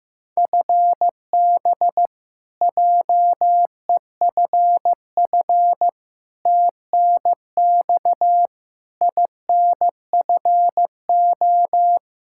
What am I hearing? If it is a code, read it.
Morse code, heard as a keyed tone, about 15 wpm, 700 Hz. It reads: FB JEFF TNX INFO
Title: Practice QSO - Exchange 3 - More Info - 15wpm